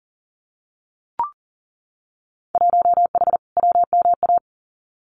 Morse code, heard as a keyed tone, about 41 wpm, 700 Hz. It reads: 1H WMA
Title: Winter Field Day Exchanges 40wpm